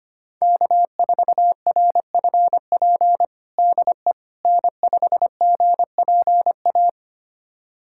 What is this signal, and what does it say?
Morse code, heard as a keyed tone, about 25 wpm, 700 Hz. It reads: K4RFP DE N5GPA
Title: USA Callsign DE Callsign - 5 Characters - 25wpm